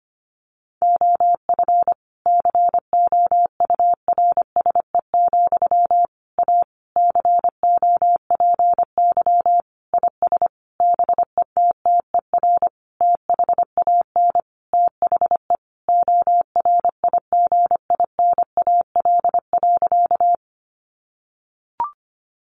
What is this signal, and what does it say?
Morse code, heard as a keyed tone, about 25 wpm, 700 Hz. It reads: OF COURSE, A COPY IS BETTER THAN THE ORIGINAL.
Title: Sentences from Top 1400 Words 25wpm